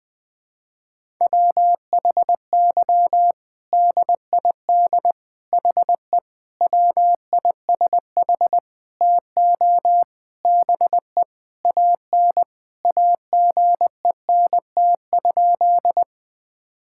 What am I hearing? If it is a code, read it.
Morse code, heard as a keyed tone, about 20 wpm, 700 Hz. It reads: WHY DID HE WISH TO BE AN AGENT?
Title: Sentences from Top 1600 Words 20wpm